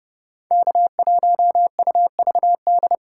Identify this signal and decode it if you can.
Morse code, heard as a keyed tone, about 31 wpm, 700 Hz. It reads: K1UVD